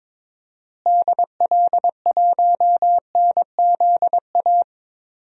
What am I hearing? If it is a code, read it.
Morse code, heard as a keyed tone, about 22 wpm, 700 Hz. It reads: DL1NZA
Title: DXCC-100 Call Signs - 6 Characters - Repeated 3x - 22wpm